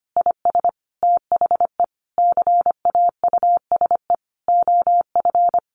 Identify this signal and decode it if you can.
Morse code, heard as a keyed tone, about 25 wpm, 700 Hz. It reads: IS THE CAUSE OF
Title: Sentences from Top 1100 Words 25wpm